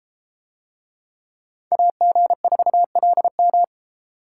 Morse code, heard as a keyed tone, about 31 wpm, 700 Hz. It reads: AG4LM